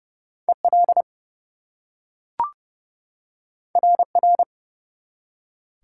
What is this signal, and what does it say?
Morse code, heard as a keyed tone, about 29 wpm, 700 Hz. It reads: EL RR
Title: Top 180 QSO Elements 30wpm